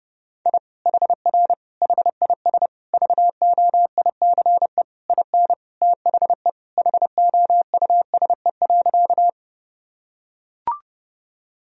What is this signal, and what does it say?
Morse code, heard as a keyed tone, about 30 wpm, 700 Hz. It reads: I HR HIS VOICE IN THE HOUSE.
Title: Sentences from Top 500 Words and Abbreviations 30wpm